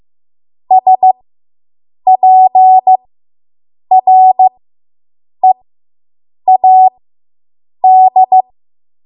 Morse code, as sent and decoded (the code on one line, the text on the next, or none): ... .--. .-. .- -..
SPREAD